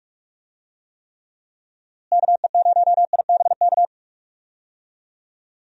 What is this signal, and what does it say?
Morse code, heard as a keyed tone, about 41 wpm, 700 Hz. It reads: KE0IBK
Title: USA Call Signs - 6 Characters - Repeated 2x - 45wpm